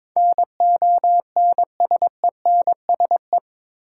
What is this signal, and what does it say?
Morse code, heard as a keyed tone, about 22 wpm, 700 Hz. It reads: NONSENSE
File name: practice / 22wpm / Eight Letter Words - 22wpm